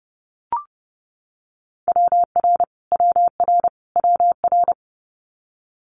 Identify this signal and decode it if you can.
Morse code, heard as a keyed tone, about 30 wpm, 700 Hz. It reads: WR WR WR
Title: Multi-letter Phonemes Repeated 3x 30wpm